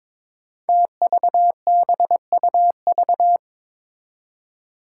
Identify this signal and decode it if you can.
Morse code, heard as a keyed tone, about 22 wpm, 700 Hz. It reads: TVBUV
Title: Five Challenging Letter-Number Combinations 22wpm